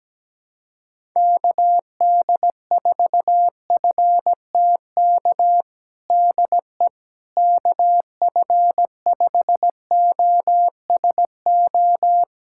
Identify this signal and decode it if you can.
Morse code, heard as a keyed tone, about 17 wpm, 700 Hz. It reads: KD4FTK DE KF5OSO
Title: USA Callsign DE Callsign 17wpm